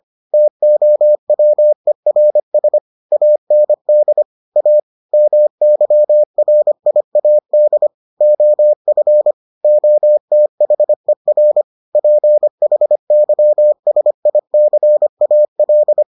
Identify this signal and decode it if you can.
Morse code, heard as a keyed tone, about 25 wpm, 600 Hz. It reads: TOWERS AND A MYRIAD OF OTHER PHYSICAL